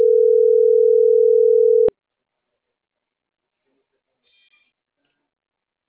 Ringback tone (North America), 440 + 480 Hz.